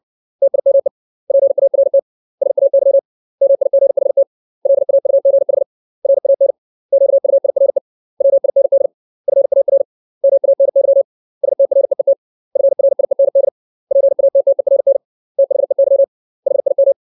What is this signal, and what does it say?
Morse code, heard as a keyed tone, about 60 wpm, 550 Hz. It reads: TIME PART SAY MIGHT LAUGH RAN QUIRE WIND FAR MANY STREET FRIEND PATTERN THEY HIM
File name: 632 words / 632 Words 60WPM